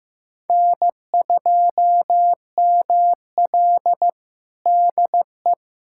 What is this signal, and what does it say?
Morse code, heard as a keyed tone, about 15 wpm, 700 Hz. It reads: N2ML DE
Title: USA Callsign DE Callsign - 4 Characters - 15wpm